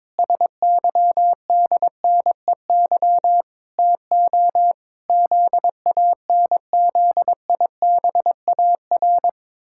Morse code, heard as a keyed tone, about 22 wpm, 700 Hz. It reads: SYDNEY TO ZANZIBAR
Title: Pangram Challenge v2 - 22wpm